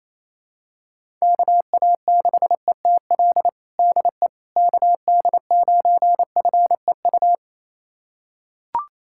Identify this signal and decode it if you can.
Morse code, heard as a keyed tone, about 28 wpm, 700 Hz. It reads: KA6ETL DE KD9FEU
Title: USA Callsign DE Callsign - 6 Characters - 28wpm